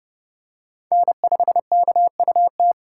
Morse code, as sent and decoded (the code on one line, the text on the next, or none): -. ..... -.- ..- -
N5KUT